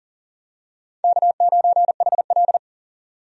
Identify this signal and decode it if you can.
Morse code, heard as a keyed tone, about 40 wpm, 700 Hz. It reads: K9HL
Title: USA Call Signs - 4 Characters - 40wpm